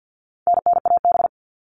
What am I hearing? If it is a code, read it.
Morse code, heard as a keyed tone, about 48 wpm, 700 Hz. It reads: NNAB
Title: All Cut Numbers - 4-Digit 50wpm